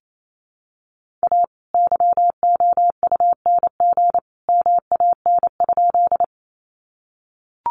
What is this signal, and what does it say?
Morse code, heard as a keyed tone, about 28 wpm, 700 Hz. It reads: A YOUNG MAN?
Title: Sentences from Top 400 Words 28wpm